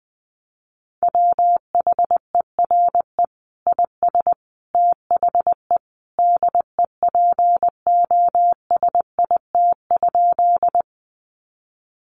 Morse code, heard as a keyed tone, about 20 wpm, 700 Hz. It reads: WHERE IS THE DEPOSIT?
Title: Sentences from Top 1300 Words 20wpm